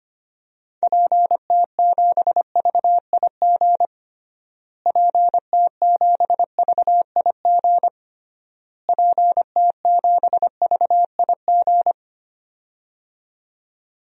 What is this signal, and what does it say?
Morse code, heard as a keyed tone, about 25 wpm, 700 Hz. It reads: PT7VIG PT7VIG PT7VIG
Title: DXCC-100 Call Signs - 6 Characters - Repeated 3x - 25wpm